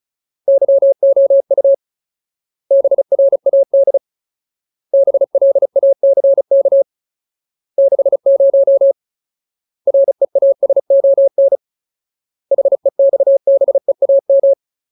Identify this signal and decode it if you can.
Morse code, heard as a keyed tone, about 35 wpm, 550 Hz. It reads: YOU BRAD BLACK 60 REASON HEXBEAM
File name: Ultimate Word List / 35WPM Ultimate